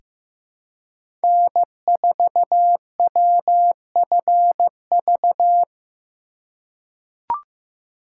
Morse code, heard as a keyed tone, about 15 wpm, 700 Hz. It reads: N4WFV
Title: USA Call Signs - 5 Characters - 2x Character Spacing - 15wpm